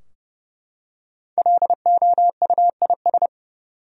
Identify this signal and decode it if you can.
Morse code, heard as a keyed tone, about 30 wpm, 700 Hz. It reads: LOUIS
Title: USA Common Names - 30wpm